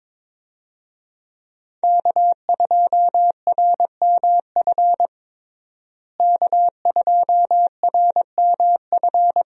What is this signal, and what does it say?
Morse code, heard as a keyed tone, about 22 wpm, 700 Hz. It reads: K2RMF K2RMF
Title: USA Call Signs - 5 Characters - Repeated 2x - 22wpm